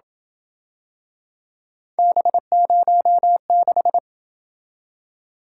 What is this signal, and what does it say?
Morse code, heard as a keyed tone, about 26 wpm, 700 Hz. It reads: B06